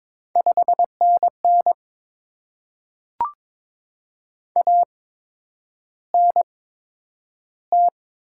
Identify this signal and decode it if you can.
Morse code, heard as a keyed tone, about 23 wpm, spaced as for 10 wpm, 700 Hz. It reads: 5NN ANT
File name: Top 20 QSO Elements - 8x Character Spacing - 22wpm